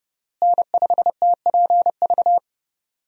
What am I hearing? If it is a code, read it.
Morse code, heard as a keyed tone, about 32 wpm, 700 Hz. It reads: N5TPV
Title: USA Call Signs - 5 Characters - Repeated 2x - 30wpm